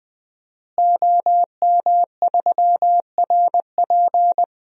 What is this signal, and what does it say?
Morse code, heard as a keyed tone, about 19 wpm, 700 Hz. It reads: OM3RP
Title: DXCC-100 Call Signs - 5 Characters - Repeated 3x - 20wpm